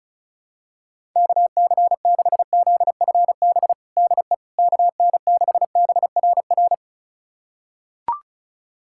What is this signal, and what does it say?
Morse code, heard as a keyed tone, about 34 wpm, 700 Hz. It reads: KC6ZFB DE KN6BRR
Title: USA Callsign DE Callsign 35wpm